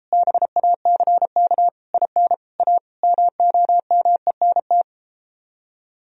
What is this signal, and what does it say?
Morse code, heard as a keyed tone, about 33 wpm, 700 Hz. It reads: BACK IN A MOMENT